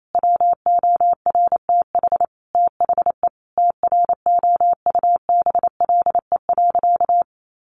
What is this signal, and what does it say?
Morse code, heard as a keyed tone, about 28 wpm, 700 Hz. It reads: WORTH THE TROUBLE.
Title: Sentences from Top 1500 Words 28wpm